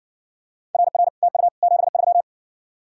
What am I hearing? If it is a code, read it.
Morse code, heard as a keyed tone, about 56 wpm, 700 Hz. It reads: RR TU 73